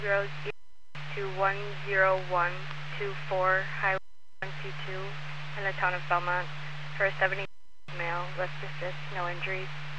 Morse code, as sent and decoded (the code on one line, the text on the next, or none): none